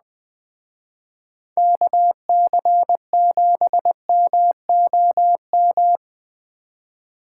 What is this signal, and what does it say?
Morse code, heard as a keyed tone, about 20 wpm, 700 Hz. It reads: KC7MOM